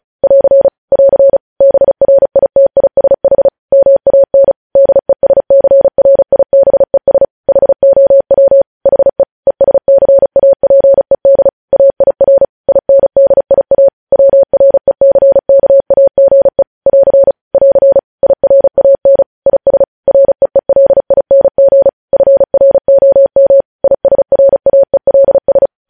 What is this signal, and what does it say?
Morse code, heard as a keyed tone, about 36 wpm, 550 Hz. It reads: <AR> <AR> BRITISH MAN DESCRIBES HOW HE ESCAPED AIR INDIA WRECKAGE <AR> <AR> IRAN IS REELING FROM ISRAELS
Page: news headlines in morse code at 35 wpm